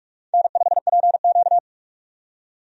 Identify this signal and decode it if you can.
Morse code, heard as a keyed tone, about 46 wpm, 700 Hz. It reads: N5PQ
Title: USA Call Signs - 4 Characters - 45wpm